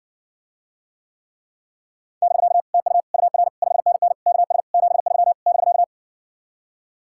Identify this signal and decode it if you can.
Morse code, heard as a keyed tone, about 60 wpm, 700 Hz. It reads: <BK> TU UR 5NN KS 73 <BK>